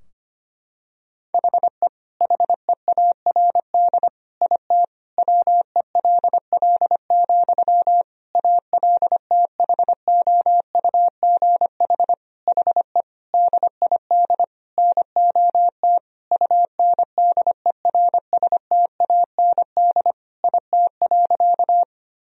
Morse code, heard as a keyed tone, about 25 wpm, 700 Hz. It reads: HE HEARD IT WELL, ALTHOUGH HE DID NOT UNDERSTAND IT.
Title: Sentences from Top 1000 Words - Review Part 3 - 25wpm